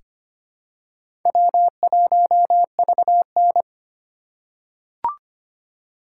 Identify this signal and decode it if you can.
Morse code, heard as a keyed tone, about 25 wpm, 700 Hz. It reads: W1VN
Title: USA Call Signs - 3x Character Spacing - 25wpm